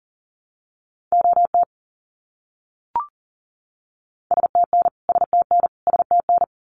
Morse code, heard as a keyed tone, about 40 wpm, 700 Hz. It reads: OT STN STN STN
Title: Abbreviations Repeated 3x - 40wpm